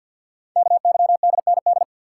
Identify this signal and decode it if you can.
Morse code, heard as a keyed tone, about 46 wpm, 700 Hz. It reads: KYDND
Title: Code Groups - 5 Letters - 50wpm